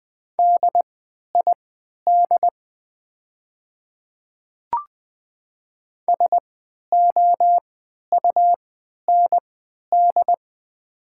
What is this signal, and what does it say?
Morse code, heard as a keyed tone, about 20 wpm, spaced as for 11 wpm, 700 Hz. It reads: DID SOUND
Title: 3x Character Spacing - Top 100 Words - 20wpm